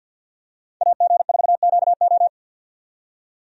Morse code, heard as a keyed tone, about 52 wpm, 700 Hz. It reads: AG4QO
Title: USA Call Signs - 5 Characters - 50wpm